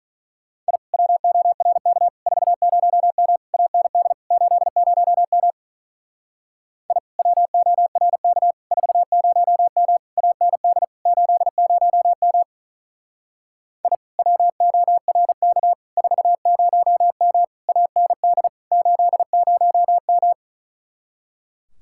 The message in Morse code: I WORK 40M AND 80M I WORK 40M AND 80M I WORK 40M AND 80M

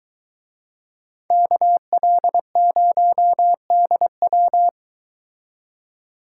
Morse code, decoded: KL0DW

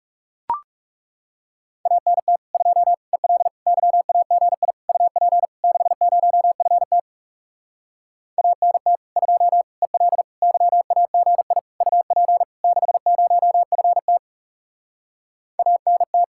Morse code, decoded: ANT 2 EL YAGI UP 60FT ANT 2 EL YAGI UP 60FT ANT

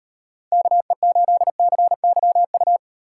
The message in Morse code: KE8CYU

